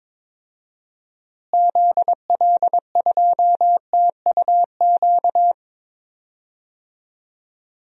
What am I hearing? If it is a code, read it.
Morse code, heard as a keyed tone, about 22 wpm, 700 Hz. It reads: ZL2TUQ
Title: DXCC-100 Call Signs - 6 Characters - Repeated 2x - 22wpm